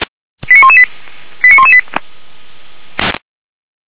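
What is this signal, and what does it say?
Five-tone selective call (ZVEI), heard as quick runs of five tones, tones of about 70 ms. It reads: 89118 89118
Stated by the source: Five-tone ZVEI selective call 89485 received on SDR.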